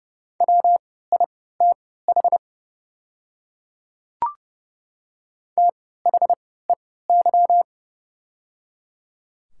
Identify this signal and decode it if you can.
Morse code, heard as a keyed tone, about 29 wpm, spaced as for 17 wpm, 700 Hz. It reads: WITH THEY
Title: 3x Character Spacing - Top 200 Words - 30wpm